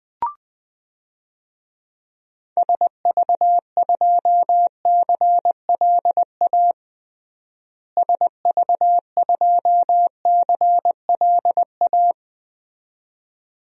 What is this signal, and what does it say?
Morse code, heard as a keyed tone, about 20 wpm, 700 Hz. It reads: SV2CLA SV2CLA
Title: DXCC-100 Call Signs - 6 Characters - Repeated 2x - 20wpm